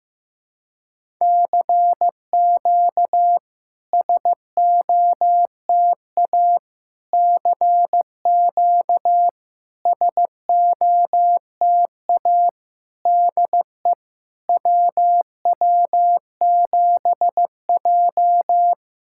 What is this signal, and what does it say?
Morse code, heard as a keyed tone, about 15 wpm, 700 Hz. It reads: CQ SOTA CQ SOTA DE WW7J
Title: SOTA QSO Practice 15wpm